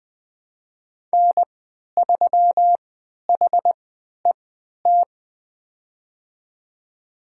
Morse code, heard as a keyed tone, about 20 wpm, spaced as for 11 wpm, 700 Hz. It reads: N3HET
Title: USA Call Signs - 5 Characters - 3x Character Spacing - 20wpm